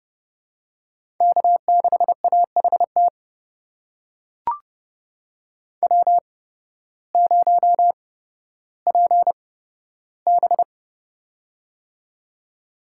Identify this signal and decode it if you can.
Morse code, heard as a keyed tone, about 30 wpm, spaced as for 14 wpm, 700 Hz. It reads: K6AHT W0PB
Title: USA Call Signs - 8x Character Spacing - 30wpm